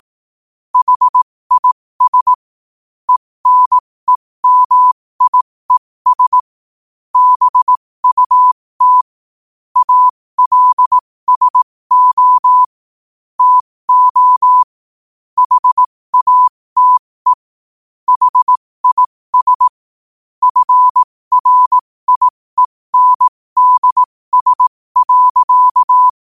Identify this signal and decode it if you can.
Morse code, heard as a keyed tone, about 18 wpm, spaced as for 15 wpm, 1000 Hz. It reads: HIS ENEMIES BUT ALSO TO HATE HIS FRIENDS.
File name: qotdfiles / 15wpm